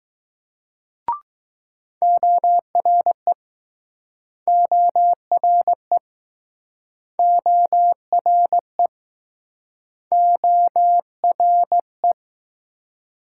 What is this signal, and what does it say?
Morse code, heard as a keyed tone, about 21 wpm, 700 Hz. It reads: ORE ORE ORE ORE